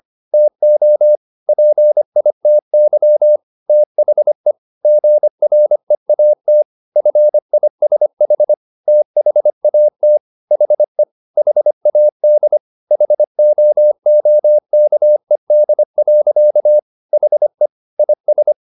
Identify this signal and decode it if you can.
Morse code, heard as a keyed tone, about 25 wpm, 600 Hz. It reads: TO PITY THE GREAT FISH THAT HE HAD HOOKED. HE IS
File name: the old man and the sea part 1 25wpm